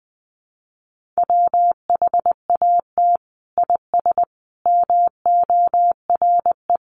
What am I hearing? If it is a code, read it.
Morse code, heard as a keyed tone, about 20 wpm, 700 Hz. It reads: WHAT IS MORE
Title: Sentences from Top 1400 Words 20wpm